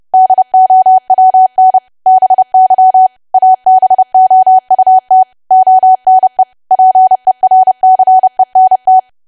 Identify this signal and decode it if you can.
Morse code, heard as a keyed tone, about 30 wpm, 750 Hz. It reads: DOWN BY ABOUT ONE PERCENT
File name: cwa / W1AW-Level3 / 30WPM